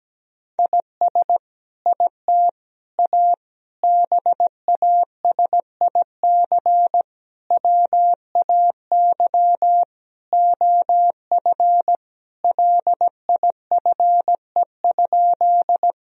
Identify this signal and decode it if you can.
Morse code, heard as a keyed tone, about 17 wpm, 700 Hz. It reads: IS IT A BASIC WAY OF LIFE?
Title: Sentences from Top 1000 Words - Review Part 3 - 17wpm